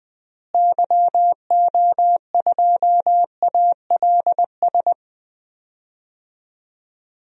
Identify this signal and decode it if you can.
Morse code, heard as a keyed tone, about 20 wpm, 700 Hz. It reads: YO2ALS
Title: DXCC-100 Call Signs - 6 Characters - Repeated 3x - 20wpm